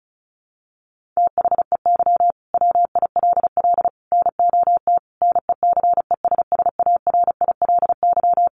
Morse code, heard as a keyed tone, about 34 wpm, 700 Hz. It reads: THEY WILL NOT NECESSARILY
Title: Sentences from Top 1400 Words 35wpm